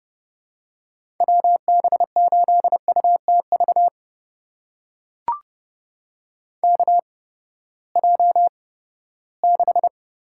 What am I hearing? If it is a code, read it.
Morse code, heard as a keyed tone, about 30 wpm, spaced as for 17 wpm, 700 Hz. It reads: WB8UTV KJ6
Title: USA Call Signs - 6 Characters - 8x Character Spacing - 30wpm